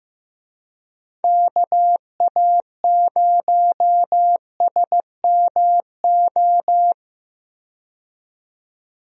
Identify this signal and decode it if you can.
Morse code, heard as a keyed tone, about 16 wpm, 700 Hz. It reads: KA0SMO
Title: USA Call Signs - 6 Characters - 15wpm